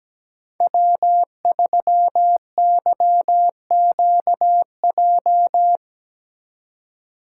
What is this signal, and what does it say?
Morse code, heard as a keyed tone, about 17 wpm, 700 Hz. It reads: W3YQJ
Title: Code Groups - 5 Letter:Numbers - 17wpm